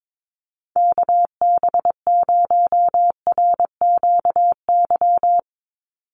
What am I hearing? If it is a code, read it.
Morse code, heard as a keyed tone, about 22 wpm, 700 Hz. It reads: KB0RQY
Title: USA Call Signs 22wpm